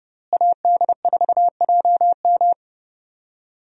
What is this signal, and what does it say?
Morse code, heard as a keyed tone, about 30 wpm, 700 Hz. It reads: AD4JM
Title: USA Call Signs - 5 Characters - Repeated 3x - 30wpm